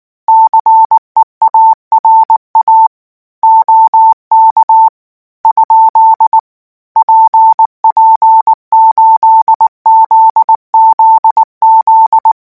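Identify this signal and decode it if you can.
Morse code, heard as a keyed tone, about 19 wpm, 890 Hz. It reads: CEARA OK ? PP8ZZZ
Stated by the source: Morse code practice text at 18 wpm, from Portal do CW.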